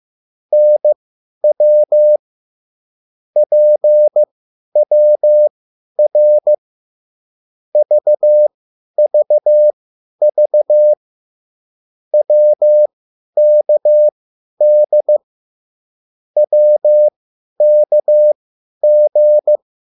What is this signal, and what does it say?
Morse code, heard as a keyed tone, about 15 wpm, spaced as for 10 wpm, 600 Hz. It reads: NW PWR VVV WKD WKG